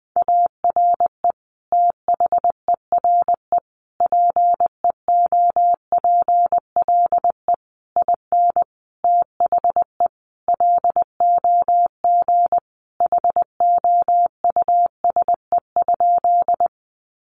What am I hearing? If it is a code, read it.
Morse code, heard as a keyed tone, about 20 wpm, 700 Hz. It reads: ARE THERE PEOPLE IN THE LOG HOUSE?
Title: Sentences from Top 1400 Words 20wpm